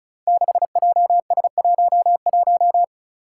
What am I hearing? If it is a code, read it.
Morse code, heard as a keyed tone, about 36 wpm, 700 Hz. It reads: 6JS11
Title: Five Challenging Letter-Number Combinations 35wpm